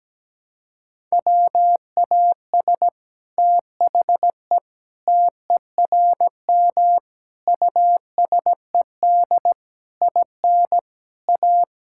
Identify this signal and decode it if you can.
Morse code, heard as a keyed tone, about 17 wpm, 700 Hz. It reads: WAS THE TERM USED IN A
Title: Sentences from Top 1000 Words - Review Part 6 - 17wpm